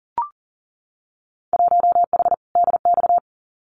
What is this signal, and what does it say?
Morse code, heard as a keyed tone, about 42 wpm, 700 Hz. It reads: 1H DX
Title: Winter Field Day Exchanges 40wpm